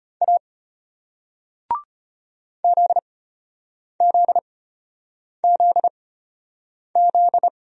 Morse code, heard as a keyed tone, about 33 wpm, spaced as for 8 wpm, 690 Hz. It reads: A ZZZZ